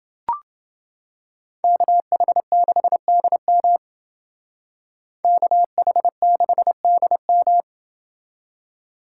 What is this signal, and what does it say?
Morse code, heard as a keyed tone, about 31 wpm, 700 Hz. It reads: KH6DM KH6DM